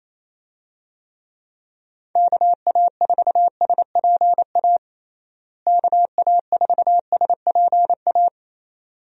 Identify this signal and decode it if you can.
Morse code, heard as a keyed tone, about 28 wpm, 700 Hz. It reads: KA4SPA KA4SPA
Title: USA Call Signs - 6 Characters - Repeated 2x - 28wpm